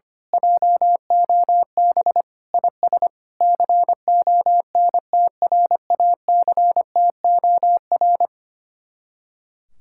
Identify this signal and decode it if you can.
Morse code, heard as a keyed tone, about 25 wpm, 700 Hz. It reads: JOB IS CONTRACTOR
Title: Speed-Racing Common QSO Phrases - 25wpm